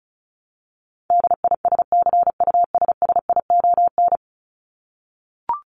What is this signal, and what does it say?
Morse code, heard as a keyed tone, about 34 wpm, 700 Hz. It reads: DISCUSSION
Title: Top 1001-1250 Words 35wpm